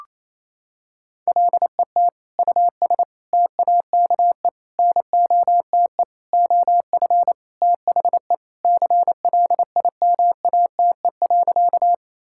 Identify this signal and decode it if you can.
Morse code, heard as a keyed tone, about 27 wpm, 700 Hz. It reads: LET US TAKE NOTE OF THE CLIMATE.
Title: Sentences from Top 1600 Words 28wpm